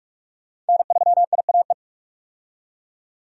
Morse code, heard as a keyed tone, about 43 wpm, 700 Hz. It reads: N3IAE
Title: USA Call Signs - 5 Characters - 45wpm